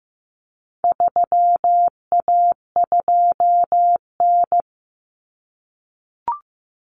Morse code, heard as a keyed tone, about 15 wpm, 700 Hz. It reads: 3A2N
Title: Common Cut Numbers - 4-Digit 15wpm